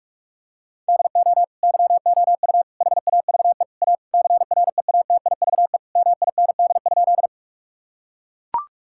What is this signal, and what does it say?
Morse code, heard as a keyed tone, about 41 wpm, 700 Hz. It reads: DO YOU HAVE A CREATIVE MIND?
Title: Sentences from Top 1000 Words - Review Part 5 - 45wpm